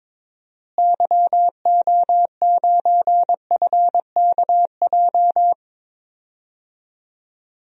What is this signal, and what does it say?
Morse code, heard as a keyed tone, about 22 wpm, 700 Hz. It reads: YO9FKJ
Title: DXCC-100 Call Signs - 6 Characters - Repeated 2x - 22wpm